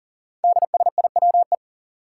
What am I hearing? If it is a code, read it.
Morse code, heard as a keyed tone, about 39 wpm, 700 Hz. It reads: DSIWE
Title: Five Challenging Letter-Number Combinations 40wpm